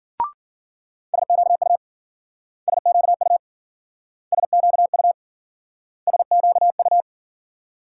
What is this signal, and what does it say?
Morse code, heard as a keyed tone, about 50 wpm, 700 Hz. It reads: SQU SQU SQU SQU
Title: Speed-Racing Multi-letter Phonemes - 40wpm